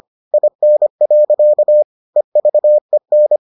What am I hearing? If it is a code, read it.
Morse code, heard as a keyed tone, about 25 wpm, 600 Hz. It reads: IN. EVEN